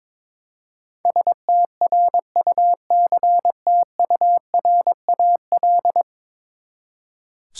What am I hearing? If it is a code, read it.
Morse code, heard as a keyed tone, about 21 wpm, 700 Hz. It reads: STRUCTURAL
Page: morse-code-ninja-practice-22wpm